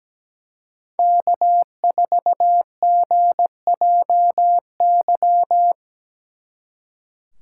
Morse code, heard as a keyed tone, about 20 wpm, 700 Hz. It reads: K4GJY
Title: Speed Racing - USA 1x3 Call Signs 17wpm